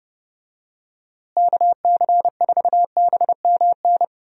Morse code, heard as a keyed tone, about 30 wpm, 700 Hz. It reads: KC4BMN
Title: USA Call Signs - 6 Characters - Repeated 3x - 30wpm